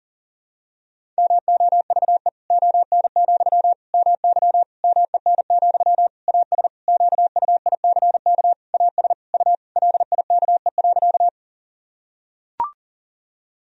Morse code, heard as a keyed tone, about 40 wpm, 700 Hz. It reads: MOVE ON, MY MEN, AS QUICK AS U LIKE.
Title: Sentences from Top 500 Words and Abbreviations 40wpm